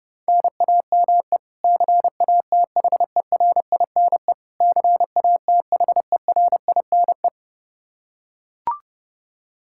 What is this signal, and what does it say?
Morse code, heard as a keyed tone, about 30 wpm, 700 Hz. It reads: NAME CATHERINE CATHERINE
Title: Practice QSO - Exchange 1 - Beginning - 30wpm